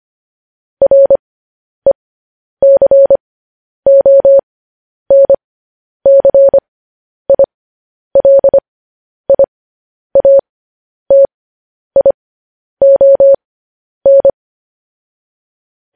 Morse code, heard as a keyed tone, about 25 wpm, spaced as for 10 wpm, 550 Hz. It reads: RECONCILIATION
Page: news-headlines-in-morse-code-at-10-wpm